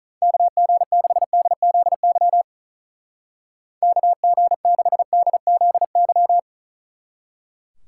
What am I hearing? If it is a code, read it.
Morse code, heard as a keyed tone, about 45 wpm, 700 Hz. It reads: KG6DZY KG6DZY